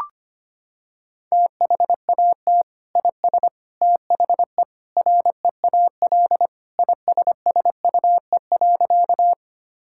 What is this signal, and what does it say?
Morse code, heard as a keyed tone, about 25 wpm, 700 Hz. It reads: THAT IS THE REAL ISSUE.
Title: Sentences from Top 1000 Words - Review Part 6 - 25wpm